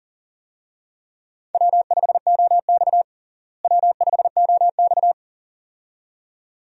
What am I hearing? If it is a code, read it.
Morse code, heard as a keyed tone, about 37 wpm, 700 Hz. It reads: W5OX W5OX
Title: USA Call Signs - 4 Characters - Repeated 2x - 40wpm